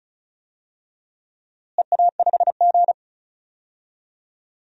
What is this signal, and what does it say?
Morse code, heard as a keyed tone, about 36 wpm, 700 Hz. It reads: EA5G
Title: DXCC-100 Call Signs - 4 Characters - Repeated 2x - 35wpm